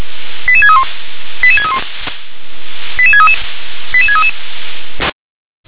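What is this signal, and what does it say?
Five-tone selective call (ZVEI), heard as quick runs of five tones, tones of about 80 ms. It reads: 88631 88631 88633 88633
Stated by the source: ZVEI five-tone selective call received on an SDR.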